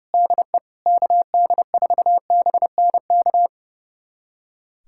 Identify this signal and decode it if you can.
Morse code, heard as a keyed tone, about 30 wpm, 700 Hz. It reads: DE KD4BNK
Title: USA Callsign DE Callsign - 6 Characters - 30wpm